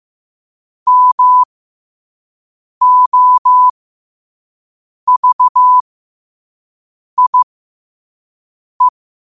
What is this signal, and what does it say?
Morse code, heard as a keyed tone, about 15 wpm, spaced as for 5 wpm, 1000 Hz. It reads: MOVIE
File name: qotdfiles / 5wpm